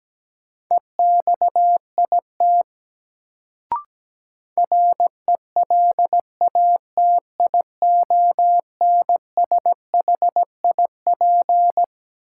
Morse code, heard as a keyed tone, about 17 wpm, 700 Hz. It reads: EXIT RELATIONSHIP
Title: Sentences from Top 1100 Words - New Word Drill Alt - 17wpm